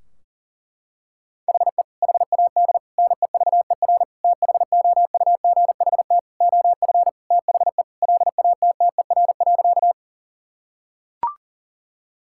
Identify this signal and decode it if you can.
Morse code, heard as a keyed tone, about 40 wpm, 700 Hz. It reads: HE HAD NEVER THOUGHT OF THE LATTER.
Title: Sentences from Top 1600 Words 40wpm